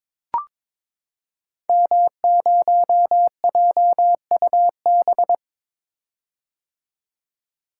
Morse code, heard as a keyed tone, about 22 wpm, 700 Hz. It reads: M0JUB
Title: Code Groups - 5 Letter:Numbers - 22wpm